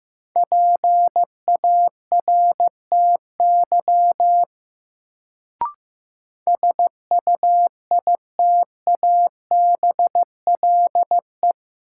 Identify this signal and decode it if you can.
Morse code, heard as a keyed tone, about 15 wpm, 700 Hz. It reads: PARTY SUITABLE
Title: Sentences from Top 1400 Words - New Word Drill - 15wpm